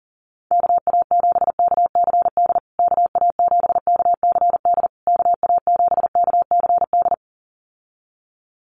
Morse code, heard as a keyed tone, about 40 wpm, 700 Hz. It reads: KA7KCD KA7KCD KA7KCD
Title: USA Call Signs Repeated 3x 40wpm